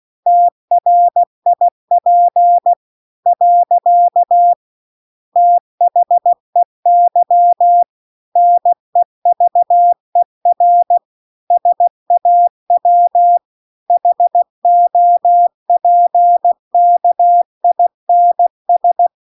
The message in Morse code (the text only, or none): TRIP . THEY NEVER SAW HOPKINS